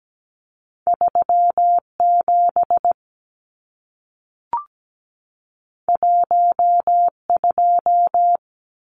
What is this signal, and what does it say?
Morse code, heard as a keyed tone, about 17 wpm, 700 Hz. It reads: 37 12
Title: Numbers - 2-Digit 17wpm